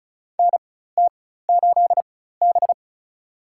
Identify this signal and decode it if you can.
Morse code, heard as a keyed tone, about 34 wpm, spaced as for 16 wpm, 700 Hz. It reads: NT8B
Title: USA Call Signs - 4x Character Spacing - 35wpm